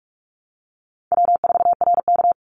Morse code, heard as a keyed tone, about 44 wpm, 700 Hz. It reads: W4RK
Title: Easy Call Signs Repeated 2x - 45wpm